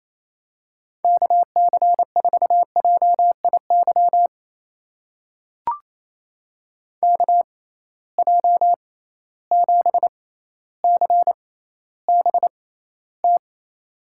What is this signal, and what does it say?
Morse code, heard as a keyed tone, about 28 wpm, spaced as for 14 wpm, 700 Hz. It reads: KC4JIY KJ7CBT
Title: USA Call Signs - 6 Characters - 6x Character Spacing - 28wpm